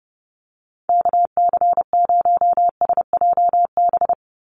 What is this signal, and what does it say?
Morse code, heard as a keyed tone, about 30 wpm, 700 Hz. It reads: KC0SJB
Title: USA Call Signs 30wpm